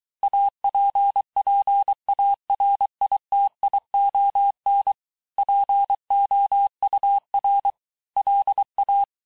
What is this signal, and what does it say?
Morse code, heard as a keyed tone, about 23 wpm, 800 Hz. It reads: APPARITION POUR LA
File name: cours-du-lundi files / cours-1852-texte-clair-vitesse-17MPM-puis-22MPM